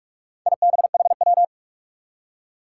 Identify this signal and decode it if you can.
Morse code, heard as a keyed tone, about 47 wpm, 700 Hz. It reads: IBHW
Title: Four Challenging Letter-Number Combinations 45wpm